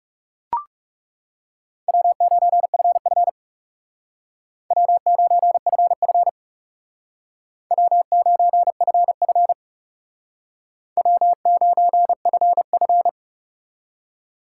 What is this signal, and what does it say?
Morse code, heard as a keyed tone, about 37 wpm, 700 Hz. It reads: W9FF W9FF W9FF W9FF
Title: Speed Racing - USA Call Signs 30wpm